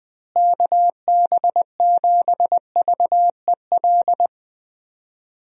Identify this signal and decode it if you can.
Morse code, heard as a keyed tone, about 20 wpm, 700 Hz. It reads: KB7VEL
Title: USA Call Signs - 6 Characters - Repeated 3x - 20wpm